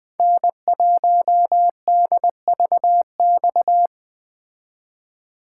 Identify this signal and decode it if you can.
Morse code, heard as a keyed tone, about 20 wpm, 700 Hz. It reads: N1DVX